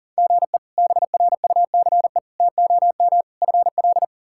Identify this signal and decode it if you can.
Morse code, heard as a keyed tone, about 40 wpm, 700 Hz. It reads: GE BRUCE TOM FL
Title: K1USN SST QSO Practice - 40wpm